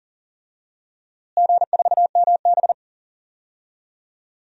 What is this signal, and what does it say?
Morse code, heard as a keyed tone, about 40 wpm, 700 Hz. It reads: G4MB
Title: DXCC-100 Call Signs - 4 Characters - Repeated 2x - 40wpm